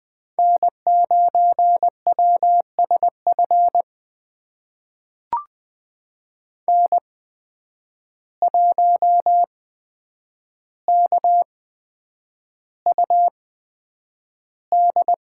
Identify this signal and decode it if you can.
Morse code, heard as a keyed tone, about 20 wpm, spaced as for 9 wpm, 700 Hz. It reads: N9WSF N1KUD
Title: USA Call Signs - 5 Characters - 8x Character Spacing - 20wpm